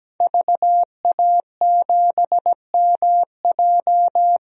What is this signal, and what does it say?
Morse code, heard as a keyed tone, about 17 wpm, 700 Hz. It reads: VA7MJ
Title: DXCC-100 Call Signs - 5 Characters - 17wpm